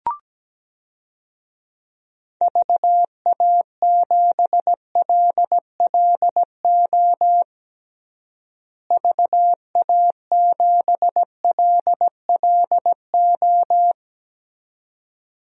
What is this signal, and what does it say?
Morse code, heard as a keyed tone, about 17 wpm, 700 Hz. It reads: VA7LLO VA7LLO
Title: DXCC-100 Call Signs - 6 Characters - Repeated 2x - 17wpm